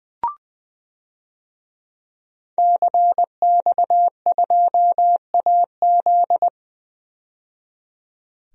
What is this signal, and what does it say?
Morse code, heard as a keyed tone, about 20 wpm, 700 Hz. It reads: CX2AZ
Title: DXCC-100 Call Signs - 5 Characters - 20wpm